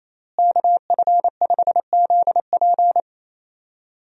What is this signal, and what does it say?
Morse code, heard as a keyed tone, about 27 wpm, 700 Hz. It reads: KF5ZP
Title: USA Call Signs - 5 Characters - Repeated 3x - 28wpm